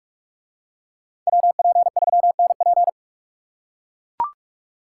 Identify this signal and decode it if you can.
Morse code, heard as a keyed tone, about 40 wpm, 700 Hz. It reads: WP3NP